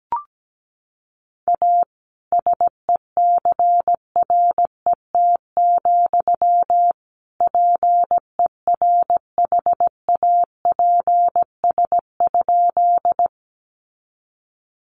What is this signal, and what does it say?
Morse code, heard as a keyed tone, about 17 wpm, 700 Hz. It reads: A SECRET, PERHAPS?
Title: Sentences from Top 1200 Words 17wpm